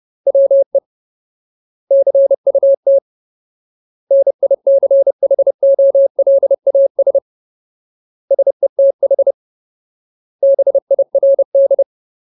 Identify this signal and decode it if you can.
Morse code, heard as a keyed tone, about 30 wpm, 550 Hz. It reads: WE CUT NICHOLAS SETH BIRD
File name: Ultimate Word List / 30WPM Ultimate